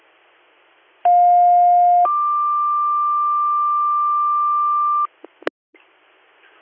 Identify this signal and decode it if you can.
Two-tone page, heard as a one-second tone, then a three-second tone, about 710 then 1180 Hz. It is heard from 1.0 to 5.1 s.